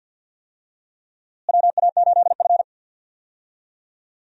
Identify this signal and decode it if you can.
Morse code, heard as a keyed tone, about 51 wpm, 700 Hz. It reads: WA8F
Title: USA Call Signs - 4 Characters - Repeated 3x - 50wpm